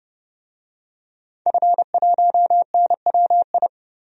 Morse code, heard as a keyed tone, about 30 wpm, 700 Hz. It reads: F1NWI
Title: DXCC-100 Call Signs - 5 Characters - 30wpm